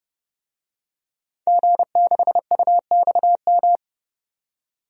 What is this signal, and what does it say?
Morse code, heard as a keyed tone, about 30 wpm, 700 Hz. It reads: G6UXM
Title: DXCC-100 Call Signs - 5 Characters - 30wpm